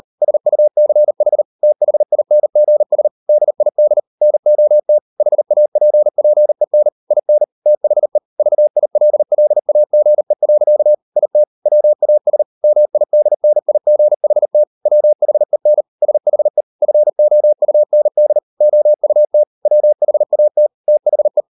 Morse code, heard as a keyed tone, about 38 wpm, 600 Hz. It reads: SUCH THINGS DID NOT HAPPEN IN THE VILLAGE. IT WAS MIDNIGHT WHEN SHE FOUND OUT WHAT THE